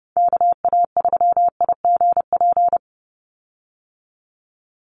Morse code, heard as a keyed tone, about 30 wpm, 700 Hz. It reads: KA3IGP
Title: USA Call Signs 30wpm